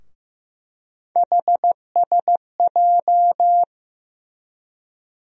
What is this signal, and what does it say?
Morse code, heard as a keyed tone, about 15 wpm, 700 Hz. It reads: HSJ
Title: USA Call Sign Postfixes - 3 Letters 15wpm